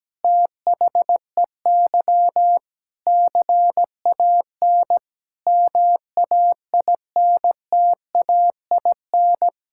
Morse code, heard as a keyed tone, about 17 wpm, 700 Hz. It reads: THEY CAN MAINTAIN
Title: Sentences from Top 1000 Words - Review Part 5 - 17wpm